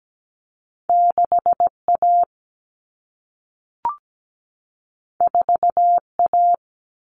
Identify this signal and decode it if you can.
Morse code, heard as a keyed tone, about 17 wpm, 700 Hz. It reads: 6A 4A
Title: Common Cut Numbers - 2-Digit 17wpm